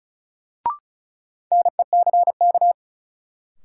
Morse code, heard as a keyed tone, about 36 wpm, 700 Hz. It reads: NECK
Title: Sets of 5 Words Encore - New Word Drill - 35wpm